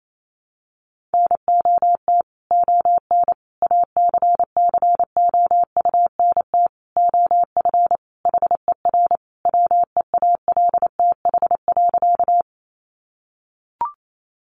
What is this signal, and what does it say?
Morse code, heard as a keyed tone, about 28 wpm, 700 Hz. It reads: NOT ON ACCOUNT OF HER WEALTH.
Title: Sentences from Top 1200 Words 28wpm